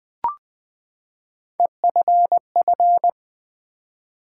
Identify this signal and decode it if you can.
Morse code, heard as a keyed tone, about 20 wpm, 700 Hz. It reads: EFF
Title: Build A Word - 3 Letters At A Time - 9-Letter Words - 20wpm